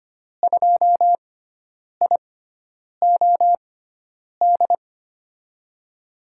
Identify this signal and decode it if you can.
Morse code, heard as a keyed tone, about 25 wpm, spaced as for 9 wpm, 700 Hz. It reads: 2IOD
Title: USA Call Signs - 5 Characters - 6x Character Spacing - 25wpm